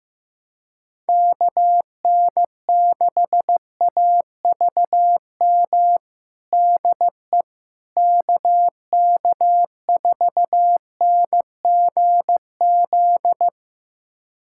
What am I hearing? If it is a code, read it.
Morse code, heard as a keyed tone, about 15 wpm, 700 Hz. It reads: KN6AVM DE KK4NGZ
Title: USA Callsign DE Callsign 15wpm